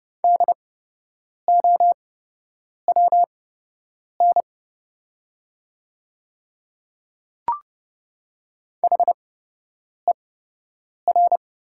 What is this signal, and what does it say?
Morse code, heard as a keyed tone, about 31 wpm, spaced as for 8 wpm, 700 Hz. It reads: DOWN HER
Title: 8x Character Spacing - Top 300 Words - 30wpm